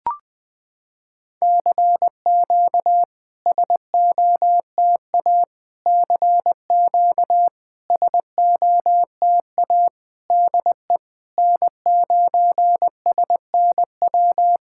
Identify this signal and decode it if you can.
Morse code, heard as a keyed tone, about 20 wpm, 700 Hz. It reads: CQ SOTA CQ SOTA DE N9SNW
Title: SOTA QSO Practice 20wpm